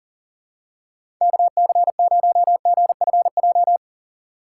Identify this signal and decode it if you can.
Morse code, heard as a keyed tone, about 39 wpm, 700 Hz. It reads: KC0GFJ
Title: USA Call Signs - 6 Characters - 2x Character Spacing - 40wpm